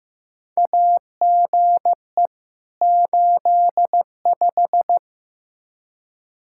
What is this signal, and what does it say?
Morse code, heard as a keyed tone, about 15 wpm, 700 Hz. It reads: AGE 85
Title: Practice QSO - Exchange 3 - More Info - 15wpm